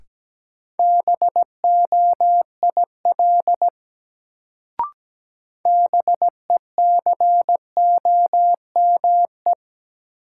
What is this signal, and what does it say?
Morse code, heard as a keyed tone, about 17 wpm, 700 Hz. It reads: BOIL BECOME
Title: Sets of 5 Words Encore - New Word Drill - 17wpm